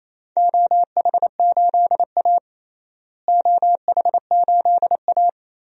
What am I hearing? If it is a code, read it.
Morse code, heard as a keyed tone, about 28 wpm, 700 Hz. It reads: OH8A OH8A
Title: DXCC-100 Call Signs - 4 Characters - Repeated 3x - 28wpm